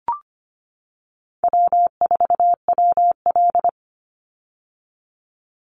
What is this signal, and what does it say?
Morse code, heard as a keyed tone, about 25 wpm, 700 Hz. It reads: W4WL
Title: USA Call Signs 25wpm